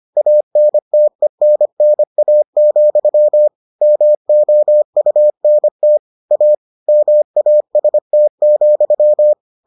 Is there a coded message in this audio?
Yes: Morse code from 0.2 to 9.3 s, heard as a keyed tone, about 25 wpm, 600 Hz. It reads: ANTENNA, MOUNT A MAST,